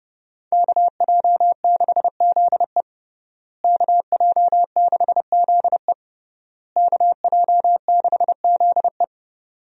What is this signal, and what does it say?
Morse code, heard as a keyed tone, about 30 wpm, 700 Hz. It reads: KJ6ZE KJ6ZE KJ6ZE
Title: USA Call Signs - 5 Characters - Repeated 3x - 30wpm